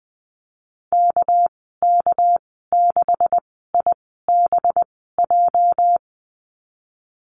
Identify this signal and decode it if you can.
Morse code, heard as a keyed tone, about 19 wpm, 700 Hz. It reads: K K 6 I B J
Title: 238 Lesson 38g - Letter q - Call Signs 20wpm